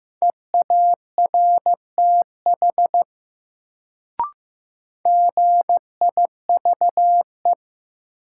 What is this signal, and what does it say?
Morse code, heard as a keyed tone, about 15 wpm, 700 Hz. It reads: EARTH GIVE
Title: Sentences from Top 200 Words - New Word Drill - 15wpm